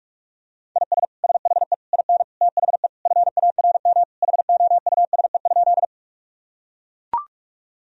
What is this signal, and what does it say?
Morse code, heard as a keyed tone, about 41 wpm, 700 Hz. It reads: IS SHE IN THE FARM HOUSE?
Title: Sentences from Top 1000 Words - Review Part 2 - 45wpm